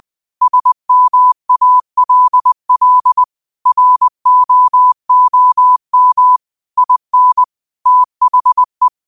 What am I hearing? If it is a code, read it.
Morse code, heard as a keyed tone, about 20 wpm, 1000 Hz. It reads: SMALL ROOM IN THE
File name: qotdfiles / 20wpm